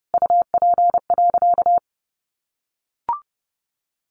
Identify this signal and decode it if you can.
Morse code, heard as a keyed tone, about 30 wpm, 700 Hz. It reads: UP.
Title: A Sound Of Thunder 30wpm